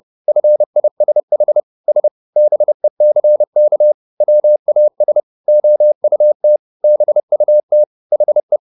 Morse code, heard as a keyed tone, about 30 wpm, 600 Hz. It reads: FISH S BECK WAS OUT BUT HE